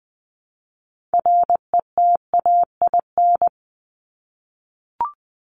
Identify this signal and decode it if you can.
Morse code, heard as a keyed tone, about 20 wpm, 700 Hz. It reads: RETAIN
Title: Top 1751-2000 Words 20wpm